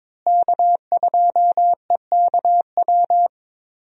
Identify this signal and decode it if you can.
Morse code, heard as a keyed tone, about 22 wpm, 700 Hz. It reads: K2EKW
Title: USA Call Signs - 5 Characters - Repeated 3x - 22wpm